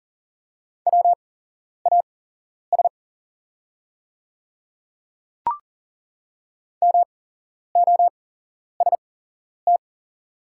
Morse code, heard as a keyed tone, about 40 wpm, spaced as for 11 wpm, 690 Hz. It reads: WAS MOST
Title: 8x Character Spacing - Top 100 Words - 40wpm